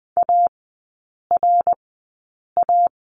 Morse code, heard as a keyed tone, about 20 wpm, spaced as for 8 wpm, 690 Hz. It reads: ARA
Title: 081 Lesson 14e - Number 2 - Three Characters 20wpm